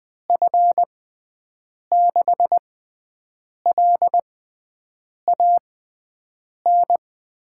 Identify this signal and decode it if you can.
Morse code, heard as a keyed tone, about 20 wpm, spaced as for 7 wpm, 700 Hz. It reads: F6LAN
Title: USA Call Signs - 6 Characters - 6x Character Spacing - 20wpm